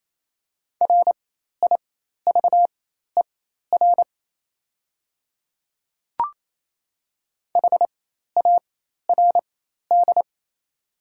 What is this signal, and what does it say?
Morse code, heard as a keyed tone, about 28 wpm, spaced as for 13 wpm, 690 Hz. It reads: RIVER HARD
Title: 4x Character Spacing - Top 500 Words - 28wpm